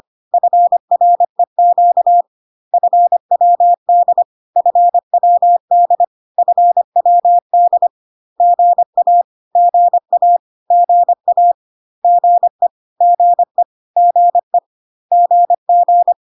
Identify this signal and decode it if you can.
Morse code, heard as a keyed tone, about 25 wpm, 700 Hz. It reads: FREQ FWD FWD FWD GA GA GA GE GE GE GG
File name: CW abbreviations alphabetical repeated 3 times 25 25wpm